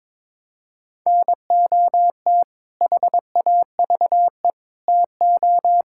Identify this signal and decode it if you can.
Morse code, heard as a keyed tone, about 21 wpm, 700 Hz. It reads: NOT HAVE TO
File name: practice / 22wpm / Build A Phrase - 5-Word Phrases - 22wpm